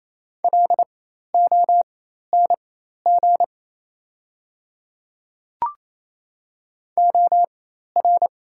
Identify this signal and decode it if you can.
Morse code, heard as a keyed tone, about 27 wpm, spaced as for 13 wpm, 690 Hz. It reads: LONG OR
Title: 4x Character Spacing - Top 100 Words - 28wpm